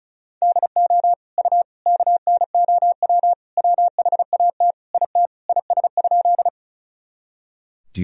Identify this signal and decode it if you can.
Morse code, heard as a keyed tone, about 36 wpm, 700 Hz. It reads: DO U KNOW WHAT IT IS?
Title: Sentences from Top 100 Words and Abbreviations 35wpm